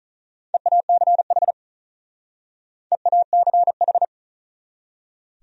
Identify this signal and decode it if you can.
Morse code, heard as a keyed tone, about 46 wpm, 700 Hz. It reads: EACH EACH